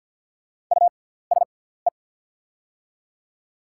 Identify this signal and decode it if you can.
Morse code, heard as a keyed tone, about 50 wpm, spaced as for 17 wpm, 700 Hz. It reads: USE
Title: 6x Character Spacing - Top 100 Words - 50wpm